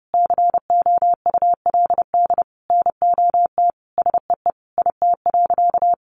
Morse code, heard as a keyed tone, about 30 wpm, 700 Hz. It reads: COULD NOT SEE IT.
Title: Sentences from Top 1000 Words 30wpm